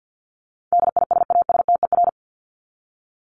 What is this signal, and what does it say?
Morse code, heard as a keyed tone, about 50 wpm, 700 Hz. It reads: DISASTER
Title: Top 1501-1750 Words 50wpm